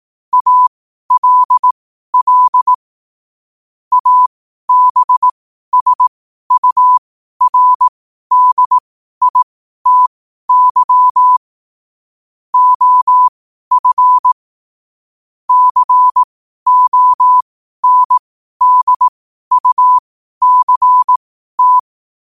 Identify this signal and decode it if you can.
Morse code, heard as a keyed tone, about 18 wpm, spaced as for 12 wpm, 1000 Hz. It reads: ALL ABSURDITY OF CONDUCT
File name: qotdfiles / 12wpm